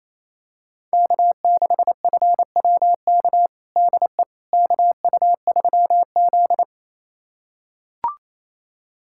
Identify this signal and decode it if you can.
Morse code, heard as a keyed tone, about 28 wpm, 700 Hz. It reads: K6FWK DE KU3Z
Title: USA Callsign DE Callsign 28wpm